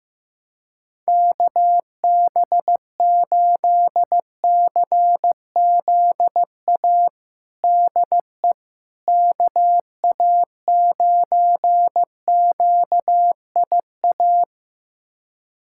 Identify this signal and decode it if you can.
Morse code, heard as a keyed tone, about 15 wpm, 700 Hz. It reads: KB8CZA DE KA9QIA
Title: USA Callsign DE Callsign 15wpm